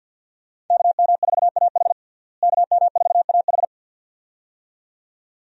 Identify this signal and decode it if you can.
Morse code, heard as a keyed tone, about 47 wpm, 700 Hz. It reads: KM4AH KM4AH
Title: USA Call Signs - 5 Characters - Repeated 2x - 50wpm